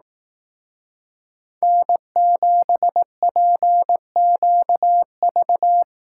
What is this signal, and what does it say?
Morse code, heard as a keyed tone, about 18 wpm, 700 Hz. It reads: N7PQV
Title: Speed Racing - USA Call Signs 15wpm